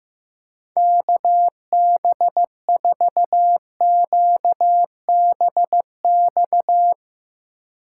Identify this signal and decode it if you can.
Morse code, heard as a keyed tone, about 16 wpm, 700 Hz. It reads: KB4QBX
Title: Speed Racing - USA 2x3 Call Signs 15wpm